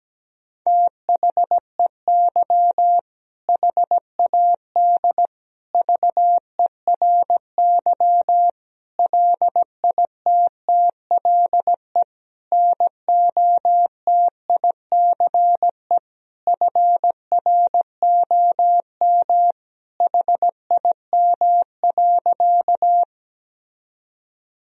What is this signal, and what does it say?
Morse code, heard as a keyed tone, about 17 wpm, 700 Hz. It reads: THEY HAD VERY LITTLE NOTICE FROM HIM.
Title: Sentences from Top 1000 Words - Review Part 2 - 17wpm